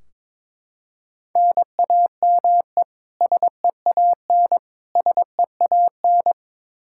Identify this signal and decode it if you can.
Morse code, heard as a keyed tone, about 22 wpm, 700 Hz. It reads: NAME SEAN SEAN
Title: Practice QSO - Exchange 1 - Beginning - 22wpm